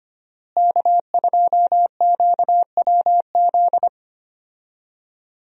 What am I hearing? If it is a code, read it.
Morse code, heard as a keyed tone, about 25 wpm, 700 Hz. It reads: K2QWZ
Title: USA Call Signs - 5 Characters - Repeated 2x - 25wpm